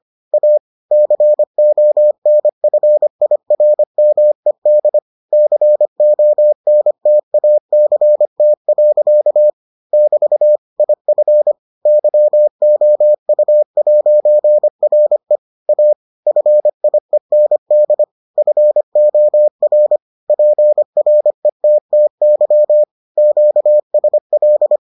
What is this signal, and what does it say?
Morse code, heard as a keyed tone, about 25 wpm, 600 Hz. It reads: A CONFIRMED CONTACT. <BT> IF YOU'RE A FIEND FOR PRETTY QSL